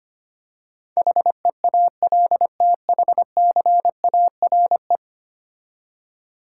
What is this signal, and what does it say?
Morse code, heard as a keyed tone, about 25 wpm, 700 Hz. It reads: HEALTHCARE
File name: 10 Letter Words - First 3 Letters Given - 25wpm